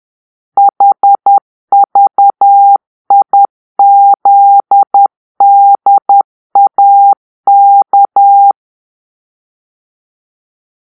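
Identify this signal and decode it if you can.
Morse code, heard as a keyed tone, about 10 wpm, 800 Hz. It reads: HVIZDAK
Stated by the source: Morse-code cipher made by a player team for the Kachna puzzle game.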